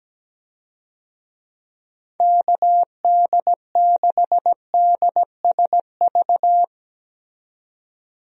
Morse code, heard as a keyed tone, about 17 wpm, 700 Hz. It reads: KD6DSV
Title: USA Call Signs - 6 Characters - 17wpm